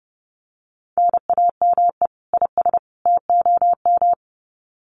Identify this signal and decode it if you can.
Morse code, heard as a keyed tone, about 30 wpm, 700 Hz. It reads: NAME IS TOM
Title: Common QSO Phrases 30wpm